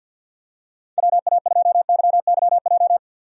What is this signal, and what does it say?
Morse code, heard as a keyed tone, about 52 wpm, 700 Hz. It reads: WA2YYJ
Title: USA Call Signs - 6 Characters - 50wpm